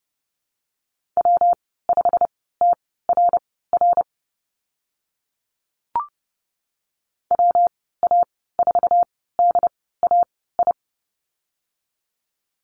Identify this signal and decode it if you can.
Morse code, heard as a keyed tone, about 30 wpm, spaced as for 17 wpm, 700 Hz. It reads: W5TRR WA4DAI
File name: Lesson 19g - Letter w - Call Signs 30wpm